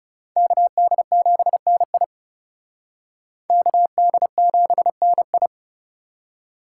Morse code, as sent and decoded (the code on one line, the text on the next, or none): -.- -.. --... -. .. / -.- -.. --... -. ..
KD7NI KD7NI